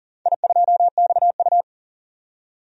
Morse code, heard as a keyed tone, about 38 wpm, 700 Hz. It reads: I2XU
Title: DXCC-100 Call Signs - 4 Characters - 40wpm